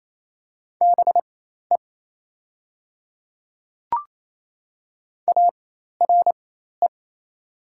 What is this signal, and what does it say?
Morse code, heard as a keyed tone, about 28 wpm, spaced as for 13 wpm, 700 Hz. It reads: BE ARE
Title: 4x Character Spacing - Top 500 Words - 28wpm